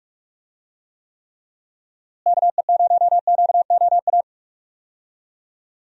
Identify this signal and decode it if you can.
Morse code, heard as a keyed tone, about 43 wpm, 700 Hz. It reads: KE0QOA
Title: USA Call Signs - 6 Characters - Repeated 3x - 45wpm